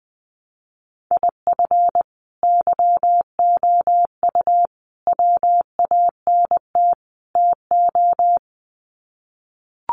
Morse code, heard as a keyed tone, about 20 wpm, 700 Hz. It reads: IF YOU WANT TO
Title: Sets of 4 Words 20wpm